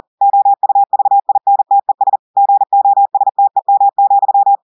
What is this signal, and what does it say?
Morse code, heard as a keyed tone, about 40 wpm, 800 Hz. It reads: OUVINTES GOSTEM,
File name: audios / portugais / portugais texte en clair 2 vitesse 40 mots minutes